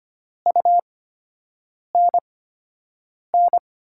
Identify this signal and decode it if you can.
Morse code, heard as a keyed tone, about 25 wpm, spaced as for 7 wpm, 700 Hz. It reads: UNN
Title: USA Call Signs - 8x Character Spacing - 25wpm